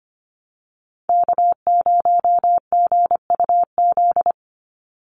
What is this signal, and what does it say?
Morse code, heard as a keyed tone, about 25 wpm, 700 Hz. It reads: K0GUZ
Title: Call Signs - 25wpm